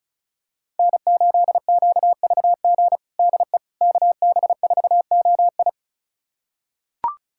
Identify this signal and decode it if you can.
Morse code, heard as a keyed tone, about 36 wpm, 700 Hz. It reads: N8QVG DE KB4OI
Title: USA Callsign DE Callsign - 5 Characters - 35wpm